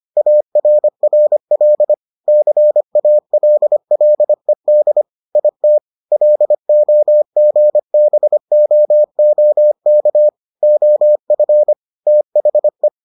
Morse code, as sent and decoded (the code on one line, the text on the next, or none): .- .-. .-. .-.. / -.-. .- .-.. .-.. -.. / .. - / .-.. --- --. -... --- --- -.- / --- ..-. / - .... .
ARRL CALLED IT LOGBOOK OF THE